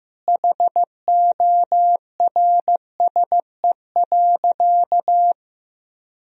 Morse code, heard as a keyed tone, about 15 wpm, 700 Hz. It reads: HORSE.
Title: Sentences from Top 1000 Words - Review Part 2 - 15wpm